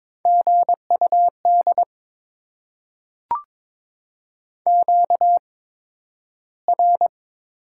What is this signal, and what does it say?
Morse code, heard as a keyed tone, about 22 wpm, spaced as for 12 wpm, 700 Hz. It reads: GUD QR
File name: Top 20 QSO Elements - 8x Character Spacing - 22wpm